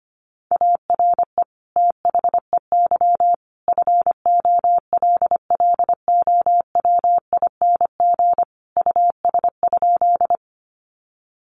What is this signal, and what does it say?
Morse code, heard as a keyed tone, about 25 wpm, 700 Hz. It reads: ARE THEY FOLLOWING US?
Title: Sentences from Top 700 Words 25wpm